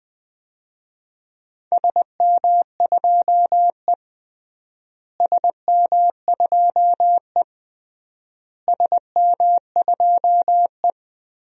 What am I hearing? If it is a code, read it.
Morse code, heard as a keyed tone, about 21 wpm, 700 Hz. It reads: SM2E SM2E SM2E
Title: DXCC-100 Call Signs - 4 Characters - Repeated 3x - 20wpm